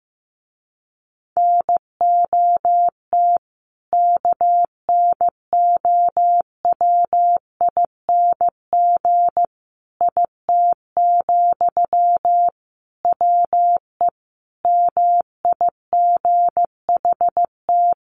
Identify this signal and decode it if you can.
Morse code, heard as a keyed tone, about 15 wpm, 700 Hz. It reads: NOT KNOWING IT, WE MIGHT
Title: A Sound Of Thunder 15wpm